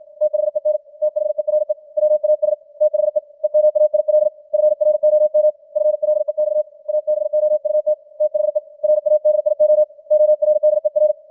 Morse code, heard as a keyed tone, about 56 wpm, 610 Hz. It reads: THEN THERE WAS THE EMAIL FROM FLEX ABOUT THE RADIO ORDER
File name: archived / CWBros